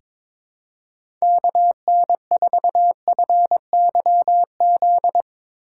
Morse code, heard as a keyed tone, about 22 wpm, 700 Hz. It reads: KN4FYZ
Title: USA Call Signs - 6 Characters - 2x Character Spacing - 22wpm